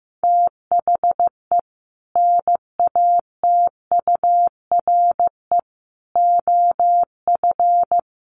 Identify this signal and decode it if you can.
Morse code, heard as a keyed tone, about 15 wpm, 700 Hz. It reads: THE NATURE OF
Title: Sets of 3 Words 15wpm